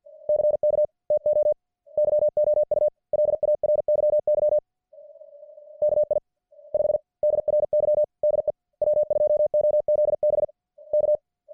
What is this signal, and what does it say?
Morse code, heard as a keyed tone, about 50 wpm, spaced as for 41 wpm, 600 Hz. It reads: CK TO YOU LARYY KI5DRY DE W2OZB K